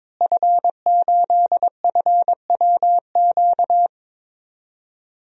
Morse code, heard as a keyed tone, about 22 wpm, 700 Hz. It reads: F8FWQ
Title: DXCC-100 Call Signs - 5 Characters - Repeated 3x - 22wpm